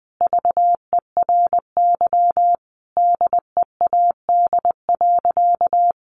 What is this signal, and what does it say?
Morse code, heard as a keyed tone, about 21 wpm, 700 Hz. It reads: VERY DEAD.
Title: A Sound Of Thunder 20wpm